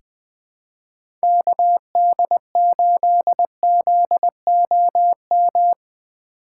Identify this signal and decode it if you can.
Morse code, heard as a keyed tone, about 21 wpm, 700 Hz. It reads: KD8ZOM
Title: USA Call Signs - 3x Character Spacing - 20wpm